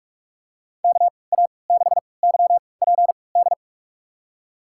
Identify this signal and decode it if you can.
Morse code, heard as a keyed tone, about 45 wpm, spaced as for 25 wpm, 700 Hz. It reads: KA6YPD
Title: USA Call Signs - 3x Character Spacing - 45wpm